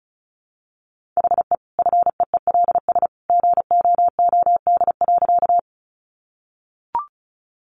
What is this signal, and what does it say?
Morse code, heard as a keyed tone, about 34 wpm, 700 Hz. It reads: HE FEELS GOOD.
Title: Additional Easy Phrases 35wpm